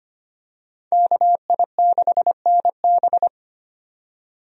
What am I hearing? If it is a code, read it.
Morse code, heard as a keyed tone, about 25 wpm, 700 Hz. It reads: KI6NB
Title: USA Call Signs - 5 Characters - 25wpm